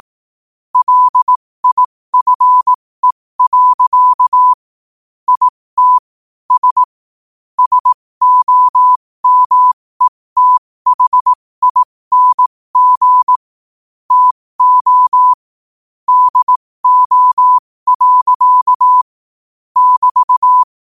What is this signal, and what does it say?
Morse code, heard as a keyed tone, about 18 wpm, spaced as for 15 wpm, 1000 Hz. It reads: LIFE. IT S SOMETHING TO DO. <BT>
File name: qotdfiles / 15wpm